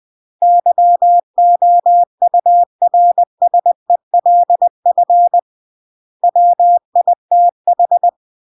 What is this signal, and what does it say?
Morse code, heard as a keyed tone, about 20 wpm, 700 Hz. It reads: YOURSELF WITH
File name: CW / jodorowsky / 82 commandments 20-20